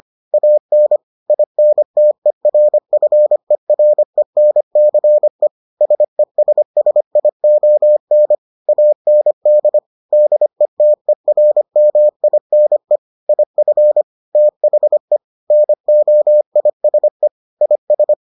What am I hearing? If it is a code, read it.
Morse code, heard as a keyed tone, about 25 wpm, 600 Hz. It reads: AN INTERFERENCE SESSION AND DETERMINE IF THE NOISE IS